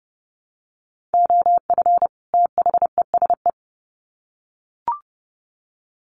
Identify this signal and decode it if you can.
Morse code, heard as a keyed tone, about 30 wpm, 700 Hz. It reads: OF THESE
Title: Sets of 2 Words 30wpm